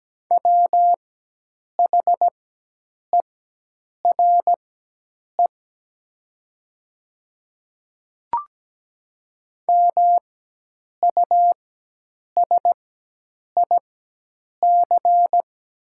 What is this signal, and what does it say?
Morse code, heard as a keyed tone, about 17 wpm, spaced as for 8 wpm, 700 Hz. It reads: WHERE MUSIC
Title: 4x Character Spacing - Top 300 Words - 17wpm